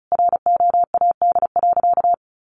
Morse code, heard as a keyed tone, about 34 wpm, 700 Hz. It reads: ROAD.
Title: Sentences from Top 400 Words 35wpm